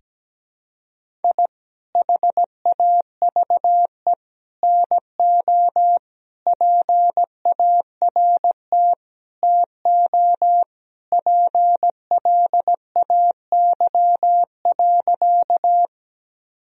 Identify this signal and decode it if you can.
Morse code, heard as a keyed tone, about 17 wpm, 700 Hz. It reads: I HAVE NO PART TO PLAY.
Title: Sentences from Top 1000 Words - Review Part 1 - 17wpm